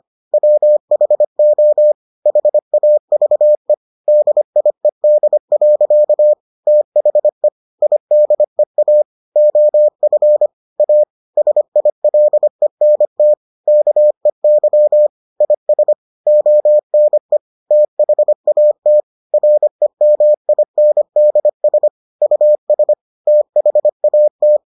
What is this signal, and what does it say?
Morse code, heard as a keyed tone, about 25 wpm, 600 Hz. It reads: WHO HAVE DIED. THE IDEA OF A SILENT KEY IS ONE THAT REMINDS US THAT